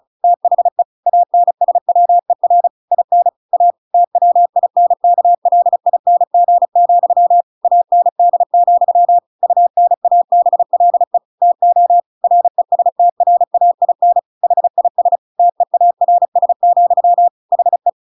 Morse code, heard as a keyed tone, about 35 wpm, 700 Hz. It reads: THE ANSWER IN A TWINKLING, AND, UNABLE TO RESTRAIN HIS TEARS, HE
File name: Pinocchio / Pinocchio ch8 35 35 ik1hge